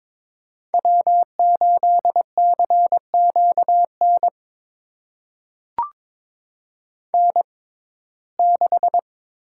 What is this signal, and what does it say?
Morse code, heard as a keyed tone, about 22 wpm, spaced as for 16 wpm, 700 Hz. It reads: W8CQN N6
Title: USA Call Signs - 5 Characters - 6x Character Spacing - 22wpm